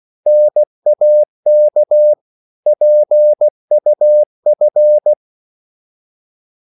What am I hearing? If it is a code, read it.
Morse code, heard as a keyed tone, about 16 wpm, 600 Hz. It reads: NAK PUF
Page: Korte danske ord 16wpm